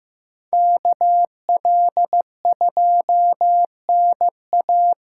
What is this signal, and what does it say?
Morse code, heard as a keyed tone, about 15 wpm, 700 Hz. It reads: KL2NA
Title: USA Call Signs - 5 Characters - 3x Character Spacing - 15wpm